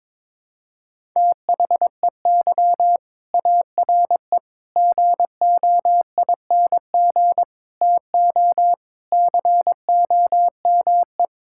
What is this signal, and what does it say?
Morse code, heard as a keyed tone, about 22 wpm, 700 Hz. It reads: THEY ARE GOING TO COME
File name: practice / 22wpm / Sentences with 3-Word Phrases - 22wpm